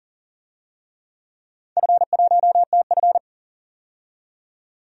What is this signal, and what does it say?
Morse code, heard as a keyed tone, about 40 wpm, 700 Hz. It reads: F1TF
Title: DXCC-100 Call Signs - 4 Characters - 40wpm